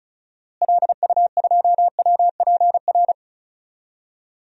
Morse code, heard as a keyed tone, about 36 wpm, 700 Hz. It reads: LU2WPR
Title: DXCC-100 Call Signs - 6 Characters - Repeated 2x - 35wpm